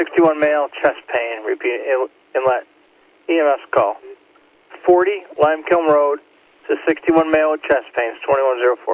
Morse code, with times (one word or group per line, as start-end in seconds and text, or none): none